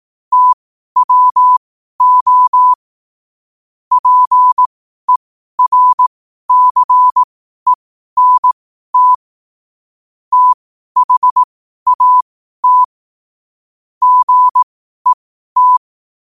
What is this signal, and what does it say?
Morse code, heard as a keyed tone, about 18 wpm, spaced as for 12 wpm, 1000 Hz. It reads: TWO PERCENT THAT GET
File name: qotdfiles / 12wpm